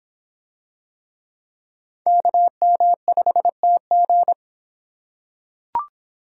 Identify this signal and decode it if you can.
Morse code, heard as a keyed tone, about 26 wpm, 700 Hz. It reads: KM5TG